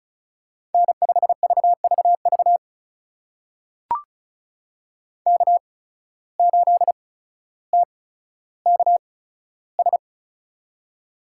Morse code, heard as a keyed tone, about 36 wpm, spaced as for 15 wpm, 700 Hz. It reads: N5VVV K8TKS
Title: USA Call Signs - 5 Characters - 8x Character Spacing - 35wpm